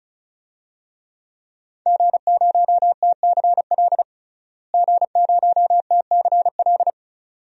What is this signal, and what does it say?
Morse code, heard as a keyed tone, about 34 wpm, 700 Hz. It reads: G0TCL G0TCL
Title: DXCC-100 Call Signs - 5 Characters - Repeated 2x - 35wpm